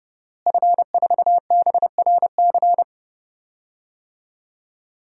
Morse code, heard as a keyed tone, about 30 wpm, 700 Hz. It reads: F4BRC
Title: DXCC-100 Call Signs - 5 Characters - Repeated 2x - 30wpm